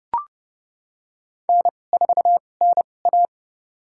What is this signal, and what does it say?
Morse code, heard as a keyed tone, about 31 wpm, spaced as for 22 wpm, 700 Hz. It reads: N4NA
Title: USA Call Signs - 4 Characters - 2x Character Spacing - 30wpm